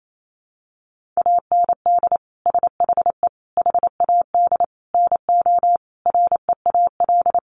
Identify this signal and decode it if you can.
Morse code, heard as a keyed tone, about 28 wpm, 700 Hz. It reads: AND SHE HAD NO REAL
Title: Sentences from Top 1200 Words 28wpm